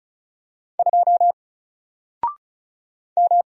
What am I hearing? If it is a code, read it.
Morse code, heard as a keyed tone, about 35 wpm, 700 Hz. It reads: 2 M
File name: Single-Letter-Number-35wpm